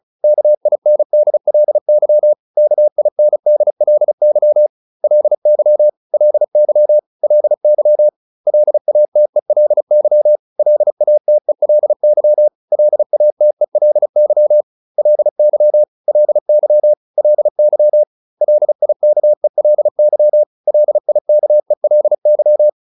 Morse code, heard as a keyed tone, about 34 wpm, 600 Hz. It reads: KINDLY KINDLY LY LY LY LATELY LATELY LATELY LY LY LY LIKELY LIKELY